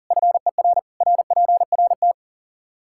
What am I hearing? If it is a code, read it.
Morse code, heard as a keyed tone, about 37 wpm, 700 Hz. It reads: FER RPRT
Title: Practice QSO - Exchange 2 - Info - 40wpm